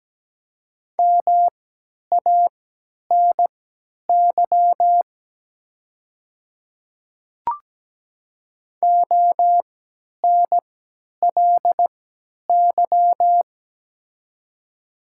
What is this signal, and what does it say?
Morse code, heard as a keyed tone, about 17 wpm, spaced as for 10 wpm, 700 Hz. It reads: MANY ONLY
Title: 3x Character Spacing - Top 200 Words - 17wpm